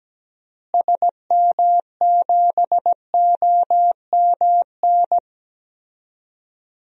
Morse code, heard as a keyed tone, about 17 wpm, 700 Hz. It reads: SM7OMN
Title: DXCC-100 Call Signs - 6 Characters - Repeated 3x - 17wpm